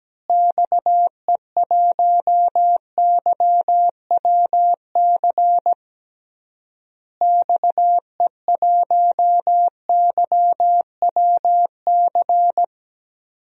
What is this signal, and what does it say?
Morse code, heard as a keyed tone, about 17 wpm, 700 Hz. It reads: XE1YWC XE1YWC
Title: DXCC-100 Call Signs - 6 Characters - Repeated 3x - 17wpm